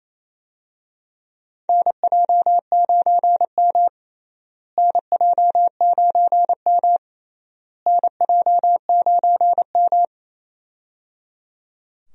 Morse code, heard as a keyed tone, about 28 wpm, 700 Hz. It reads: NJ9M NJ9M NJ9M
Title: USA Call Signs - 4 Characters - Repeated 3x - 28wpm